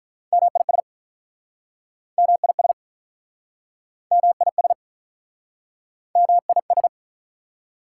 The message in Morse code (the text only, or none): MIS MIS MIS MIS